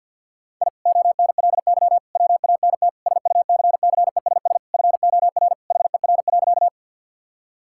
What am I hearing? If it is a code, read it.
Morse code, heard as a keyed tone, about 50 wpm, 700 Hz. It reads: I ONLY WANT SUCCESS FOR HER.
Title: Sentences from Top 1000 Words - Review Part 6 - 50wpm